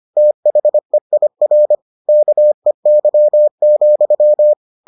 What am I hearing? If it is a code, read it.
Morse code, heard as a keyed tone, about 25 wpm, 600 Hz. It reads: THEIR KEY,